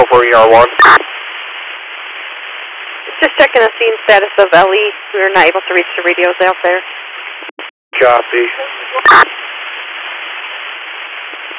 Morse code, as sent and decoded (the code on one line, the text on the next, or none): none